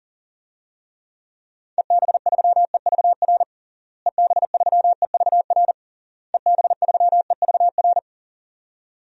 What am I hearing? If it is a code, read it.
Morse code, heard as a keyed tone, about 40 wpm, 700 Hz. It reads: EB3EVR EB3EVR EB3EVR
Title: DXCC-100 Call Signs - 6 Characters - Repeated 3x - 40wpm